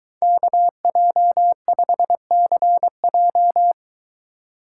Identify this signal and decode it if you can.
Morse code, heard as a keyed tone, about 22 wpm, 700 Hz. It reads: KJ5CJ